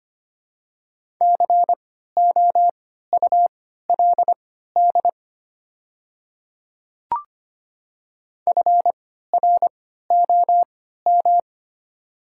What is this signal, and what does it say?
Morse code, heard as a keyed tone, about 25 wpm, spaced as for 14 wpm, 690 Hz. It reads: COULD FROM
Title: 3x Character Spacing - Top 100 Words - 25wpm